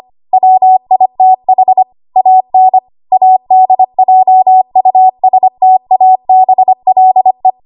Morse code, heard as a keyed tone, about 25 wpm, 750 Hz. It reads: WITH AN ADJUSTABLE
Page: W1AW Web Code Practice Files